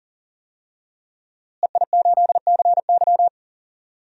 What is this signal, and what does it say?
Morse code, heard as a keyed tone, about 40 wpm, 700 Hz. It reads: EI8CY
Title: DXCC-100 Call Signs - 5 Characters - Repeated 3x - 40wpm